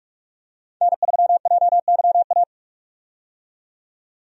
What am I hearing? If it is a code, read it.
Morse code, heard as a keyed tone, about 46 wpm, 700 Hz. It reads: N3JYA
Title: USA Call Signs - 5 Characters - 45wpm